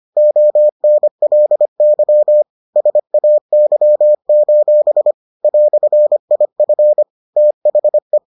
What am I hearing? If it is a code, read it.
Morse code, heard as a keyed tone, about 25 wpm, 600 Hz. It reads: ONLY SAY: "IF THE